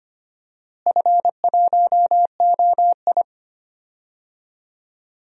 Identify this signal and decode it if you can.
Morse code, heard as a keyed tone, about 25 wpm, 700 Hz. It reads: F1OI
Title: DXCC-100 Call Signs - 4 Characters - Repeated 2x - 25wpm